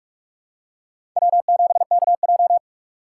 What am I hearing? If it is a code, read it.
Morse code, heard as a keyed tone, about 48 wpm, 700 Hz. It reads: W7KJ
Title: USA Call Signs - 3x Character Spacing - 45wpm